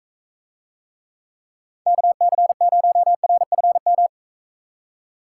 Morse code, heard as a keyed tone, about 44 wpm, 700 Hz. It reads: KC0RFM